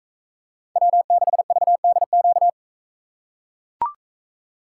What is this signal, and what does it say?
Morse code, heard as a keyed tone, about 43 wpm, 700 Hz. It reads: W6VDQ